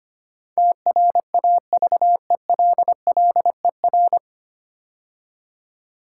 Morse code, heard as a keyed tone, about 25 wpm, 700 Hz. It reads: TRAVELLER
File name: practice / 25wpm / 9 Letter Words - First 3 Letters Given - 25wpm